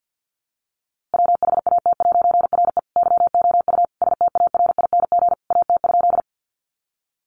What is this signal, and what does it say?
Morse code, heard as a keyed tone, about 50 wpm, 700 Hz. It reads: WHAT'RE YOU STARING AT?
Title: A Sound Of Thunder 50wpm